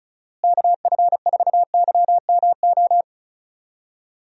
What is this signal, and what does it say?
Morse code, heard as a keyed tone, about 34 wpm, 700 Hz. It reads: KF4YMO
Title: USA Call Signs - 6 Characters - Repeated 3x - 35wpm